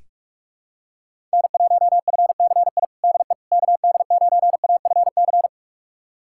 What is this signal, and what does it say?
Morse code, heard as a keyed tone, about 41 wpm, 700 Hz. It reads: N1FCI DE KD9AFC
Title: USA Callsign DE Callsign 45wpm